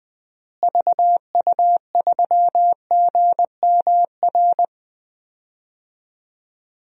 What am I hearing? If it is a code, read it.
Morse code, heard as a keyed tone, about 20 wpm, 700 Hz. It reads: VU3GMR
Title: DXCC-100 Call Signs - 6 Characters - Repeated 2x - 20wpm